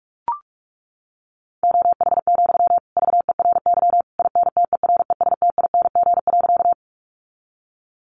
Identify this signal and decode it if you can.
Morse code, heard as a keyed tone, about 44 wpm, 700 Hz. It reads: OH, VERY INTERESTING.
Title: Sentences from Top 1000 Words 45wpm